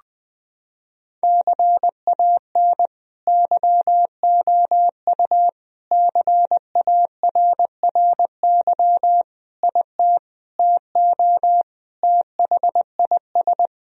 Morse code, heard as a keyed tone, about 20 wpm, 700 Hz. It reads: CAN YOU CARRY IT TO THIS
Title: Sentences from Top 1000 Words - Review Part 1 - 20wpm